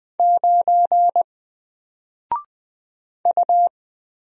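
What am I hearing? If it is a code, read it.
Morse code, heard as a keyed tone, about 20 wpm, 700 Hz. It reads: 9 U
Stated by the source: Single letter and number Morse drill at 20 wpm, from N4LCD.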